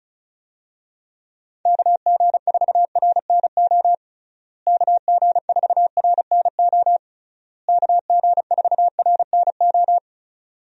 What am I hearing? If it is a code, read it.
Morse code, heard as a keyed tone, about 36 wpm, 700 Hz. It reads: KG4RNO KG4RNO KG4RNO
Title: USA Call Signs - 6 Characters - Repeated 3x - 35wpm